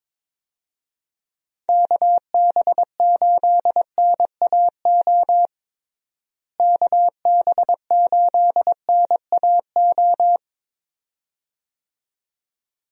Morse code, heard as a keyed tone, about 22 wpm, 700 Hz. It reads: KB8NAO KB8NAO
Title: USA Call Signs - 6 Characters - Repeated 2x - 22wpm